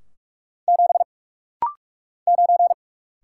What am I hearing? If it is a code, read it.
Morse code, heard as a keyed tone, about 44 wpm, 700 Hz. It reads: 7 9
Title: Single Letter-Number Alt - Rapid-Fire 45wpm